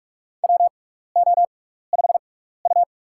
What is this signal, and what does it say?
Morse code, heard as a keyed tone, about 46 wpm, spaced as for 16 wpm, 700 Hz. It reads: WO5U
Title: USA Call Signs - 4 Characters - 6x Character Spacing - 45wpm